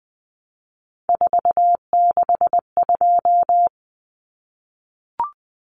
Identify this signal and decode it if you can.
Morse code, heard as a keyed tone, about 20 wpm, 700 Hz. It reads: 462
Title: Common Cut Numbers - 3-Digit 20wpm